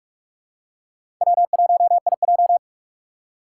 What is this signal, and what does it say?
Morse code, heard as a keyed tone, about 45 wpm, 700 Hz. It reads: W1IJ
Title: USA Call Signs - 4 Characters - 2x Character Spacing - 45wpm